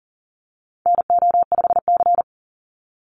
Morse code, heard as a keyed tone, about 40 wpm, 700 Hz. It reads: NO5C
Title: USA Call Signs 40wpm